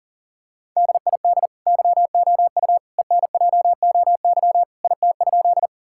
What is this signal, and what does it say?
Morse code, heard as a keyed tone, about 37 wpm, 700 Hz. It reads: DID YOU ENJOY IT?
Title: Sentences from Top 1000 Words - Review Part 5 - 40wpm